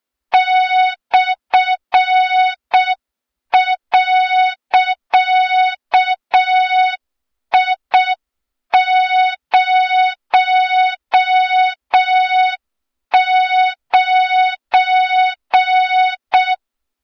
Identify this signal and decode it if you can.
Morse code, heard as a keyed tone, about 6 wpm, 3700 Hz. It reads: /.I09
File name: Morze-HA2MN / hang / 23 GYAKORLAT